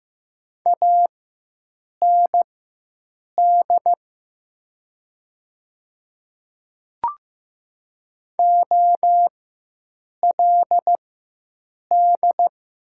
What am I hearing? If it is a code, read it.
Morse code, heard as a keyed tone, about 15 wpm, spaced as for 7 wpm, 700 Hz. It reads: AND OLD
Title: 4x Character Spacing - Top 200 Words - 15wpm